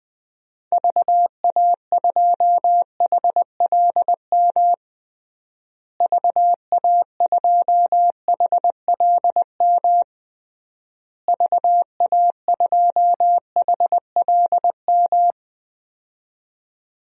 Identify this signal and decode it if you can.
Morse code, heard as a keyed tone, about 20 wpm, 700 Hz. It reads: VA2HLM VA2HLM VA2HLM
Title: DXCC-100 Call Signs - 6 Characters - Repeated 3x - 20wpm